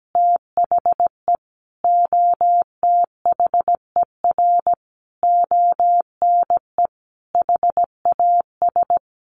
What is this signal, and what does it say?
Morse code, heard as a keyed tone, about 17 wpm, 700 Hz. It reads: THE OTHER ONE HAS
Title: Additional Easy Phrases 17wpm